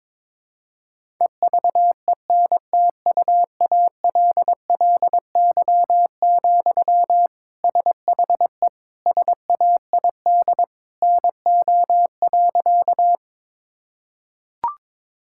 Morse code, heard as a keyed tone, about 22 wpm, 700 Hz. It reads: EVENTUALLY, SHE SAID NO.
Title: Sentences from Top 1000 Words - Review Part 6 - 22wpm